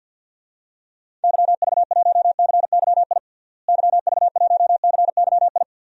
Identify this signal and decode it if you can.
Morse code, heard as a keyed tone, about 53 wpm, 700 Hz. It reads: YV1CYI YV1CYI
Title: DXCC-100 Call Signs - 6 Characters - Repeated 2x - 50wpm